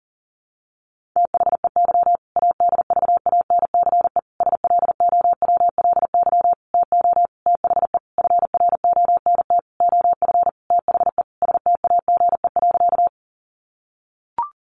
Morse code, heard as a keyed tone, about 37 wpm, 700 Hz. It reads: THEY ADVANCE SLOWLY TO THE FRONT OF THE STAGE.
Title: Sentences from Top 1400 Words 40wpm